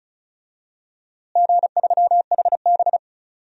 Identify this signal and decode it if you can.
Morse code, heard as a keyed tone, about 36 wpm, 700 Hz. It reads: G3HB
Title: DXCC-100 Call Signs - 4 Characters - Repeated 2x - 35wpm